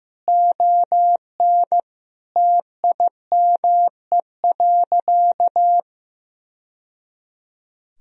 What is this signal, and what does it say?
Morse code, heard as a keyed tone, about 15 wpm, 700 Hz. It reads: ON TIME.
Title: Sentences from Top 1000 Words - Review Part 2 - 15wpm